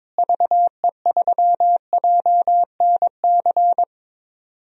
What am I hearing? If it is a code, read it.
Morse code, heard as a keyed tone, about 21 wpm, 700 Hz. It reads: VE3JNC
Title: DXCC-100 Call Signs - 6 Characters - Repeated 2x - 22wpm